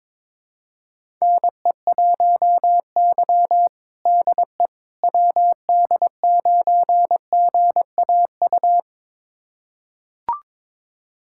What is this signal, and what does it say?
Morse code, heard as a keyed tone, about 22 wpm, 700 Hz. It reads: NE1Y DE WD9GAU
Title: USA Callsign DE Callsign 22wpm